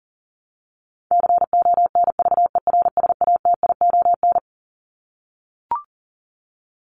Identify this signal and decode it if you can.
Morse code, heard as a keyed tone, about 40 wpm, 700 Hz. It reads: CONVERSATION
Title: Top 1751-2000 Words 40wpm